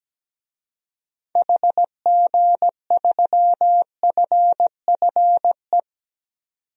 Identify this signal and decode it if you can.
Morse code, heard as a keyed tone, about 17 wpm, 700 Hz. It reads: HG3FFE
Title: DXCC-100 Call Signs - 6 Characters - 17wpm